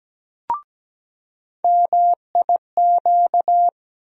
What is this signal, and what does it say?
Morse code, heard as a keyed tone, about 17 wpm, 700 Hz. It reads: MIQ
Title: USA Call Sign Postfixes - 3 Letters 17wpm